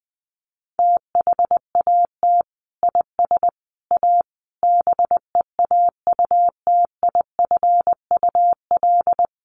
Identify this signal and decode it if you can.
Morse code, heard as a keyed tone, about 20 wpm, 700 Hz. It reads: THAT IS A BEAUTIFUL
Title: Sentences from Top 1500 Words 20wpm